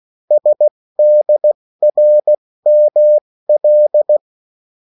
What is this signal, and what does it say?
Morse code, heard as a keyed tone, about 16 wpm, spaced as for 14 wpm, 600 Hz. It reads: SDRML
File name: morse mp3 / revision9-vitesse 14 mots